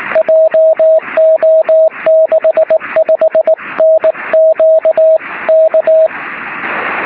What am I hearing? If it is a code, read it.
Morse code, heard as a keyed tone, about 19 wpm, 640 Hz, with radio noise. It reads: JO65NQ K
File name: sm7lcb / 03-sm7ecm 23cm